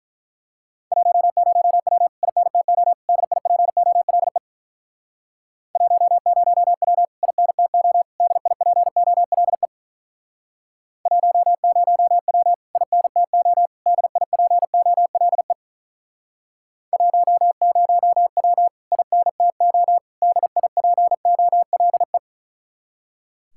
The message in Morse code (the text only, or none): 10W INTO DIPOLE 10W INTO DIPOLE 10W INTO DIPOLE 10W INTO DIPOLE